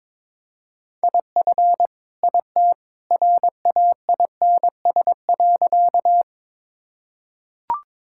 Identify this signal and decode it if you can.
Morse code, heard as a keyed tone, about 22 wpm, 700 Hz. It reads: IF IT RAINS.
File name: practice / 22wpm / Sentences with 2-Word Phrases - 22wpm